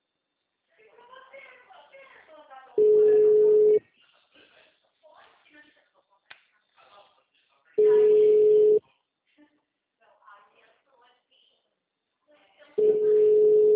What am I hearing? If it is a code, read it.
Ringback tone (Europe, 425 Hz), with speech.